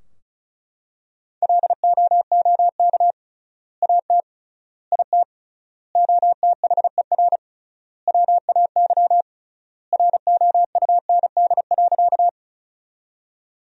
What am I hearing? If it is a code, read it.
Morse code, heard as a keyed tone, about 36 wpm, 700 Hz. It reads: LOOK AT IT OTHER WAY ROUND.
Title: 3x Word-Spacing - Sentences from Top 200 Words - 35wpm